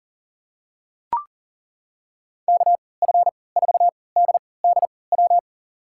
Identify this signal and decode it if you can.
Morse code, heard as a keyed tone, about 40 wpm, spaced as for 23 wpm, 700 Hz. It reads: KF4DDW
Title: USA Call Signs - 6 Characters - 3x Character Spacing - 40wpm